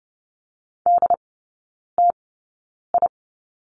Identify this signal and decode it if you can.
Morse code, heard as a keyed tone, about 30 wpm, spaced as for 9 wpm, 700 Hz. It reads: DTI
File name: Lesson 16e - Letter c - Three Characters 30wpm